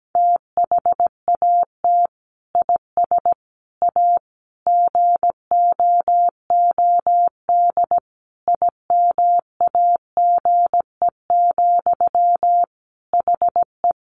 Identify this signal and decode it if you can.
Morse code, heard as a keyed tone, about 17 wpm, 700 Hz. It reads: THAT IS A GOOD IMAGE, HE
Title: Sentences from Top 700 Words 17wpm